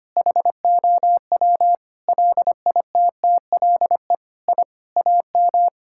Morse code, heard as a keyed tone, about 25 wpm, 700 Hz. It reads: HOW LITTLE I AM
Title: Sentences from Top 1000 Words - Review Part 6 - 25wpm